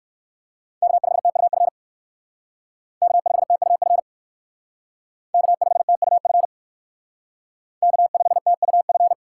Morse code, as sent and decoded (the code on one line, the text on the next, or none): -.- ..... - ..- ..-. / -.- ..... - ..- ..-. / -.- ..... - ..- ..-. / -.- ..... - ..- ..-.
K5TUF K5TUF K5TUF K5TUF